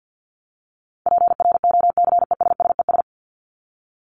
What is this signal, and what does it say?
Morse code, heard as a keyed tone, about 50 wpm, 700 Hz. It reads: PROCESSES
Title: ES:S Suffix - Top 100 Words with Base Words 50wpm